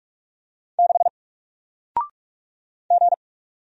Morse code, heard as a keyed tone, about 44 wpm, 700 Hz. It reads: 6 G
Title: Single Letter-Number Alt 45wpm